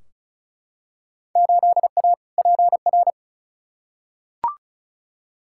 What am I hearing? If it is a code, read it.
Morse code, heard as a keyed tone, about 36 wpm, 690 Hz. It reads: 8A PR
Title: Field Day Exchanges Encore - 35wpm